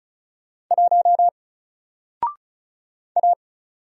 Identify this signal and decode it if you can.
Morse code, heard as a keyed tone, about 34 wpm, 700 Hz. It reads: 1 A
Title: Single Letter-Number Alt 35wpm